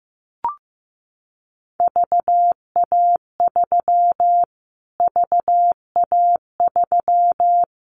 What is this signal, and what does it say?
Morse code, heard as a keyed tone, about 15 wpm, 700 Hz. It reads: VA3 VA3
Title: Callsign Prefixes for Top 25 DXCC - Repeated 3x - 15wpm